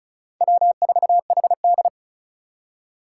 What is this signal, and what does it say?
Morse code, heard as a keyed tone, about 36 wpm, 700 Hz. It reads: W4HD
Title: Four Challenging Letter-Number Combinations 35wpm